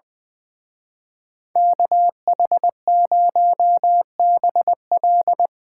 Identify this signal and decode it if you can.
Morse code, heard as a keyed tone, about 22 wpm, 700 Hz. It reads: KH0BL
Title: Speed Racing - USA 2x2 Call Signs 20wpm